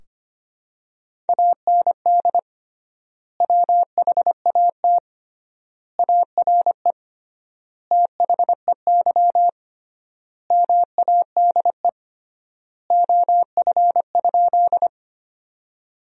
Morse code, heard as a keyed tone, about 25 wpm, 700 Hz. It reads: AND WHAT ARE THEY MADE OF?
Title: 3x Word-Spacing - Sentences from Top 200 Words - 25wpm